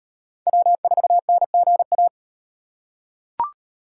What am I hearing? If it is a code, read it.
Morse code, heard as a keyed tone, about 36 wpm, 700 Hz. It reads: W4NGA